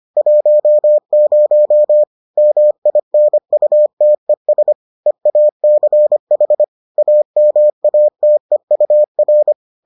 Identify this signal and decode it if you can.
Morse code, heard as a keyed tone, about 25 wpm, 600 Hz. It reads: 10 MINUTES EACH AMATEUR